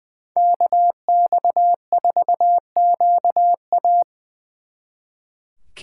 Morse code, heard as a keyed tone, about 24 wpm, 700 Hz. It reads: KX4QA
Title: Speed Racing - USA 2x2 Call Signs 20wpm